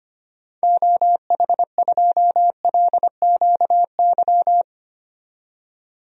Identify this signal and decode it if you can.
Morse code, heard as a keyed tone, about 25 wpm, 700 Hz. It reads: OH2LQY
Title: DXCC-100 Call Signs - 6 Characters - 25wpm